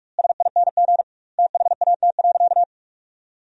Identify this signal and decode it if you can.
Morse code, heard as a keyed tone, about 46 wpm, 700 Hz. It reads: SING THAT.
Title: 2x Word-Spacing - Sentences from Top 500 Words - 45wpm